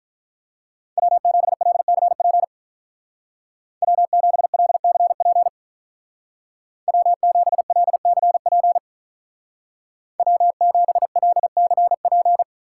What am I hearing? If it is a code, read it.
Morse code, heard as a keyed tone, about 44 wpm, 700 Hz. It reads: W7LCP W7LCP W7LCP W7LCP